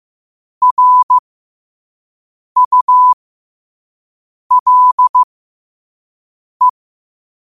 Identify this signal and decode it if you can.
Morse code, heard as a keyed tone, about 15 wpm, spaced as for 5 wpm, 1000 Hz. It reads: RULE
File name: qotdfiles / 5wpm